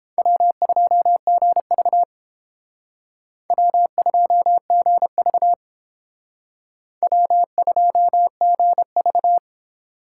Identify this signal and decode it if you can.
Morse code, heard as a keyed tone, about 29 wpm, 700 Hz. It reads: W2GV W2GV W2GV